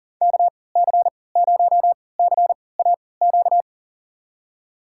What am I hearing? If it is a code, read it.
Morse code, heard as a keyed tone, about 40 wpm, spaced as for 23 wpm, 700 Hz. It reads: KC0CAQ
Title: USA Call Signs - 6 Characters - 3x Character Spacing - 40wpm